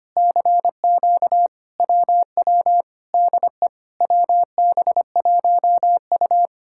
Morse code, heard as a keyed tone, about 24 wpm, 700 Hz. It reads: CQ WW DE WB1U
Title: Speed-Racing Common QSO Phrases Encore - 25wpm